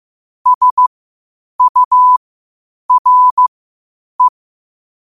Morse code, heard as a keyed tone, about 15 wpm, spaced as for 8 wpm, 1000 Hz. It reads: SURE
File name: qotdfiles / 8wpm